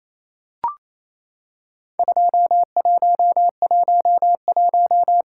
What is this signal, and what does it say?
Morse code, heard as a keyed tone, about 28 wpm, 700 Hz. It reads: 2111
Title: Worked All Europe DX Contest - QTC Exchange - 28wpm